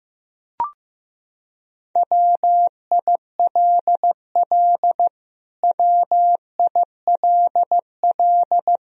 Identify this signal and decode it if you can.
Morse code, heard as a keyed tone, about 16 wpm, 700 Hz. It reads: WILL WILL
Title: Top 100 Words Repeated 15wpm